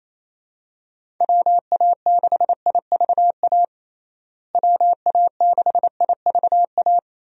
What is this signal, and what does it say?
Morse code, heard as a keyed tone, about 28 wpm, 700 Hz. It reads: WA6IVA WA6IVA
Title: USA Call Signs - 6 Characters - Repeated 3x - 28wpm